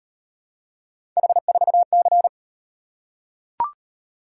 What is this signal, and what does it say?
Morse code, heard as a keyed tone, about 38 wpm, 700 Hz. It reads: H4C